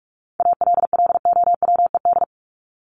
Morse code, heard as a keyed tone, about 44 wpm, 700 Hz. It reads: ALLOWED
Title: ED Suffix - Top 100 Words with Base Words 45wpm